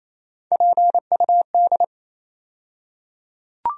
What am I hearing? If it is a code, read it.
Morse code, heard as a keyed tone, about 27 wpm, 700 Hz. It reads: PUD
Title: USA Call Sign Postfixes - 3 Letters 28wpm